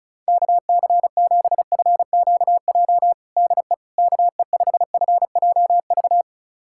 Morse code, heard as a keyed tone, about 35 wpm, 700 Hz. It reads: KC7FQJ DE KE5FJV
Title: USA Callsign DE Callsign - 6 Characters - 35wpm